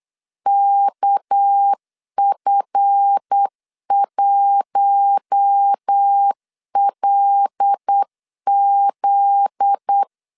Morse code, heard as a keyed tone, about 8 wpm, 800 Hz. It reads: KF1LZ
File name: CW / DET / 1 9 1500C S C 8wpm 5653